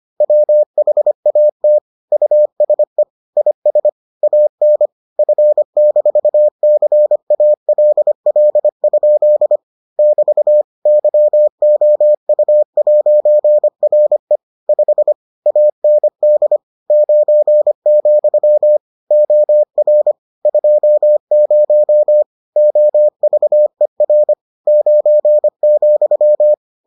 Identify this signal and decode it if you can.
Morse code, heard as a keyed tone, about 25 wpm, 600 Hz. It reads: WHAT USE IS AN F-CALL? <BT> YOU'RE 5 AND 9, OR 20 OVER 9,